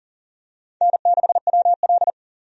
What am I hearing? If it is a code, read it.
Morse code, heard as a keyed tone, about 40 wpm, 700 Hz. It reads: N6WL
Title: USA Call Signs - 4 Characters - Repeated 3x - 40wpm